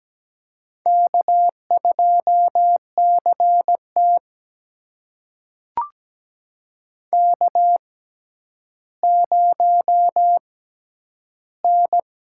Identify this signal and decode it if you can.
Morse code, heard as a keyed tone, about 17 wpm, spaced as for 10 wpm, 700 Hz. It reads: K2CT K0N
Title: USA Call Signs - 4 Characters - 6x Character Spacing - 17wpm